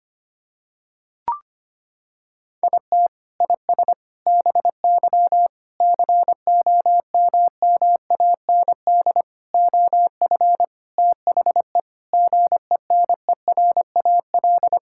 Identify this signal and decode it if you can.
Morse code, heard as a keyed tone, about 25 wpm, 700 Hz. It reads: IT IS BY COMMAND OF THE GENERAL
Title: Sentences from Top 1600 Words 25wpm